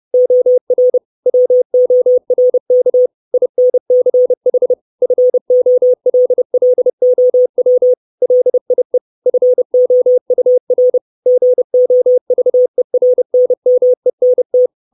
Morse code, heard as a keyed tone, about 30 wpm, 500 Hz. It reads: OR WORK INCH FOLLOW LIE FOUR GOVERNMENT
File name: 632 Words 30WPM